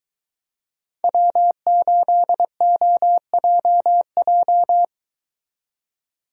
Morse code, heard as a keyed tone, about 22 wpm, 700 Hz. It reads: W8OJJ